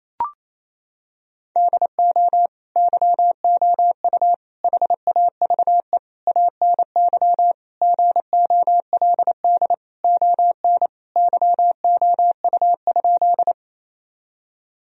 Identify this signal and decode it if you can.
Morse code, heard as a keyed tone, about 28 wpm, 700 Hz. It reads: DO YOU HAVE ANY GOLD ON YOU?
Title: Sentences from Top 1000 Words - Review Part 3 - 28wpm